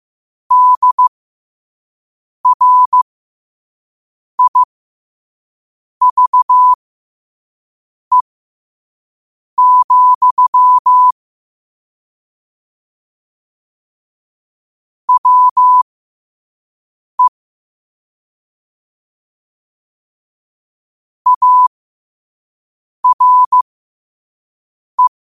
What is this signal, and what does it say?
Morse code, heard as a keyed tone, about 15 wpm, spaced as for 5 wpm, 1000 Hz. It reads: DRIVE, WE ARE
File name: qotdfiles / 5wpm